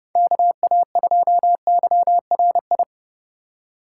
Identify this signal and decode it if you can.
Morse code, heard as a keyed tone, about 30 wpm, 700 Hz. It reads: KA2YRI
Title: USA Call Signs - 2x Character Spacing - 30wpm